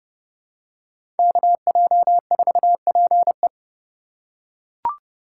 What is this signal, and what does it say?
Morse code, heard as a keyed tone, about 30 wpm, 700 Hz. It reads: KJ4PE
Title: USA Call Signs - 5 Characters - 3x Character Spacing - 30wpm